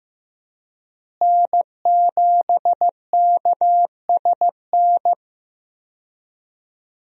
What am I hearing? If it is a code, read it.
Morse code, heard as a keyed tone, about 15 wpm, 700 Hz. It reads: N7KSN
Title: USA Call Signs - 5 Characters - Repeated 2x - 15wpm